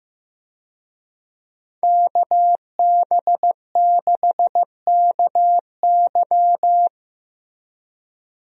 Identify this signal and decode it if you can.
Morse code, heard as a keyed tone, about 16 wpm, 700 Hz. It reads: KB6KY
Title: USA Call Signs - 5 Characters - Repeated 3x - 15wpm